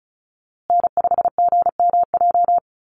Morse code, heard as a keyed tone, about 34 wpm, 700 Hz. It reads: N5GMJ
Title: USA Call Signs Repeated 2x 35wpm